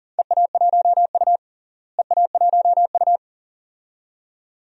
Morse code, heard as a keyed tone, about 40 wpm, 700 Hz. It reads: EA1U EA1U
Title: DXCC-100 Call Signs - 4 Characters - Repeated 2x - 40wpm